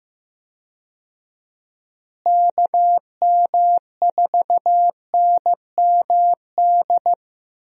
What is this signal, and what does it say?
Morse code, heard as a keyed tone, about 15 wpm, 700 Hz. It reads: KM4NMD
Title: USA Call Signs - 6 Characters - Repeated 2x - 15wpm